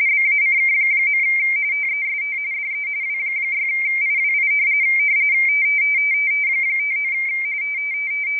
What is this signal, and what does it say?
RTTY, about 45.45 baud. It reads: BACK. TEST DE W6IWI 59
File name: rtty / websdr recording 14086.0kHz